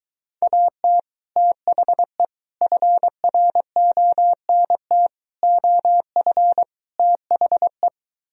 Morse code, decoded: AT THE FRONT OF THE

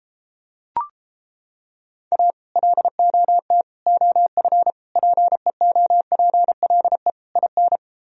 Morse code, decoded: A LOT OF PEOPLE IN

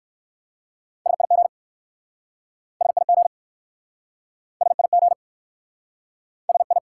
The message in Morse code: SIG SIG SIG SI

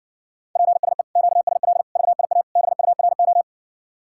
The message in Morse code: PSE QSL VIA BURO

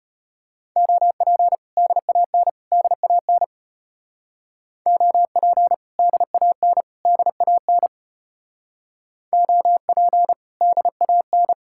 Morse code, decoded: OP DAN DAN OP DAN DAN OP DAN

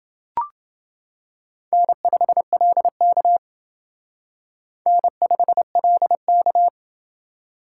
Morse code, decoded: N5LK N5LK

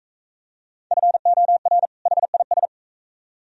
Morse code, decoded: FOR HIS

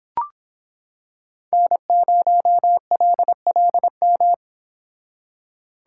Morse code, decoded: N0LLM

